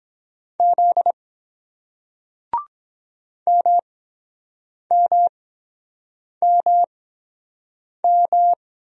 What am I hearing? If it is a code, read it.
Morse code, heard as a keyed tone, about 24 wpm, spaced as for 7 wpm, 700 Hz. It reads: Z MMMM